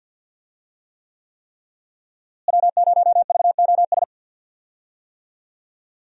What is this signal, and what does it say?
Morse code, heard as a keyed tone, about 52 wpm, 700 Hz. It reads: W0VOS
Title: USA Call Signs - 5 Characters - Repeated 3x - 50wpm